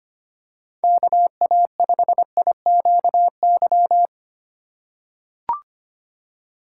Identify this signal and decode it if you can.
Morse code, heard as a keyed tone, about 25 wpm, 700 Hz. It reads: KA5IQY
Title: USA Call Signs - 2x Character Spacing - 25wpm